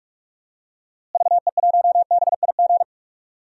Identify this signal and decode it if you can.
Morse code, heard as a keyed tone, about 46 wpm, 700 Hz. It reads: VE1BIG
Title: DXCC-100 Call Signs - 6 Characters - Repeated 3x - 45wpm